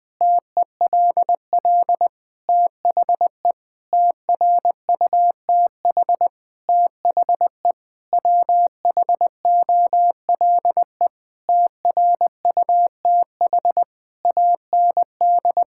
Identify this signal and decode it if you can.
Morse code, heard as a keyed tone, about 21 wpm, 700 Hz. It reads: TELL THE TRUTH THE WHOLE TRUTH AND
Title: Sentences with 9-Word Phrases - Spoken Beginning - 20wpm